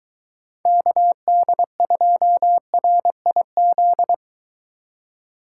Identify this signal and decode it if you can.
Morse code, heard as a keyed tone, about 22 wpm, 700 Hz. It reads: KD2RIZ